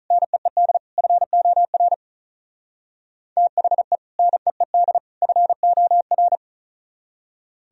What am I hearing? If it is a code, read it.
Morse code, heard as a keyed tone, about 44 wpm, 700 Hz. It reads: NEED FOR THE NEED FOR